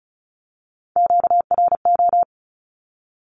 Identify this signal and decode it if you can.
Morse code, heard as a keyed tone, about 34 wpm, 700 Hz. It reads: QRO
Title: Q-Codes 35wpm